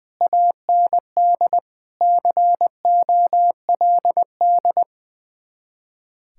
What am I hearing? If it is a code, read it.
Morse code, heard as a keyed tone, about 20 wpm, 700 Hz. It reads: AND COLD
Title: Speed-Racing Common QSO Phrases Encore - 20wpm